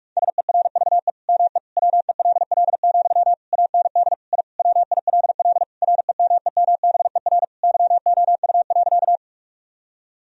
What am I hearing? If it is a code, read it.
Morse code, heard as a keyed tone, about 46 wpm, 700 Hz. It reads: SERVE ME WELL, AND I WILL REMEMBER YOU.
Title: Sentences from Top 1000 Words - Review Part 2 - 45wpm